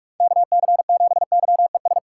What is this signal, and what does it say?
Morse code, heard as a keyed tone, about 42 wpm, 700 Hz. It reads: KC7YES
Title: USA Call Signs - 6 Characters - 2x Character Spacing - 45wpm